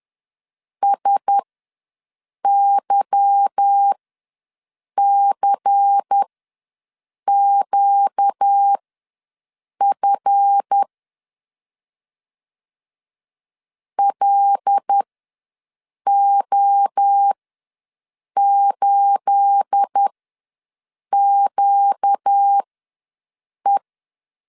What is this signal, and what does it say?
Morse code, heard as a keyed tone, about 10 wpm, spaced as for 6 wpm, 800 Hz. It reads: SYCQF LO8QE
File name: LEZIONI / 9 28 ws 10wpm 4124